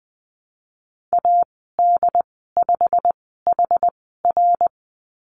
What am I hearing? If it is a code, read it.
Morse code, heard as a keyed tone, about 20 wpm, spaced as for 15 wpm, 700 Hz. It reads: AD5HR
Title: 088 Lesson 15g - Number 5 - Call Signs 20wpm